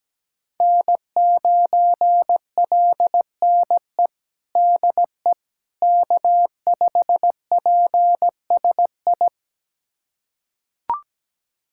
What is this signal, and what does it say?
Morse code, heard as a keyed tone, about 17 wpm, 700 Hz. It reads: N9LNE DE K5PSI
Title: USA Callsign DE Callsign - 5 Characters - 17wpm